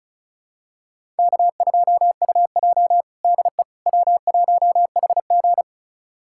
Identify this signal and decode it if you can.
Morse code, heard as a keyed tone, about 36 wpm, 700 Hz. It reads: K2UJ DE W1HG
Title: USA Callsign DE Callsign - 4 Characters - 35wpm